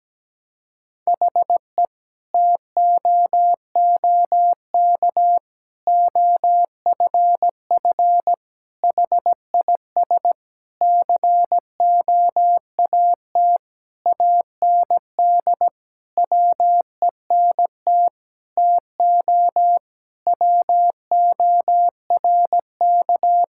Morse code, heard as a keyed tone, about 17 wpm, 700 Hz. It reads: HE TOOK OFF HIS COAT AND WENT TO WORK
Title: Sentences from Top 1600 Words 17wpm